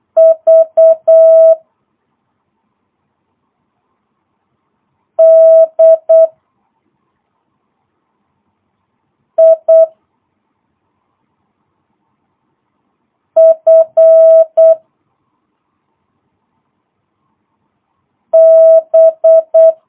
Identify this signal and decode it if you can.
Morse code, heard as a keyed tone, about 8 wpm, 650 Hz. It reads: V D I F B
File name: jh8cbh / 2022cw